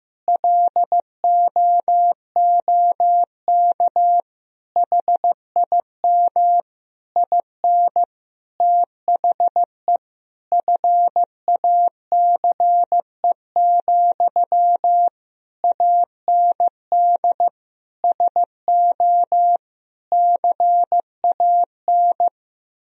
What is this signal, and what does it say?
Morse code, heard as a keyed tone, about 15 wpm, 700 Hz. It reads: LOOK HIM IN THE FACE, AND SO CAN
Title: Sentences from Top 1000 Words - Review Part 2 - 15wpm